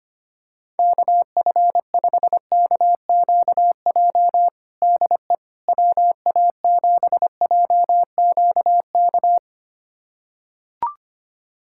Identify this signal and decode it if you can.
Morse code, heard as a keyed tone, about 25 wpm, 700 Hz. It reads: KF5KQJ DE WA7JQK
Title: USA Callsign DE Callsign 25wpm